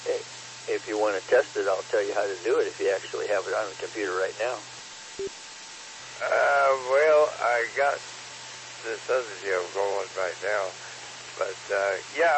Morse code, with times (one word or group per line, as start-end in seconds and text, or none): none